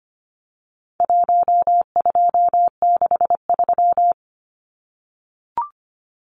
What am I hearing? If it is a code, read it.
Morse code, heard as a keyed tone, about 25 wpm, 700 Hz. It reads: 1263
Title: Numbers - 4-Digit 25wpm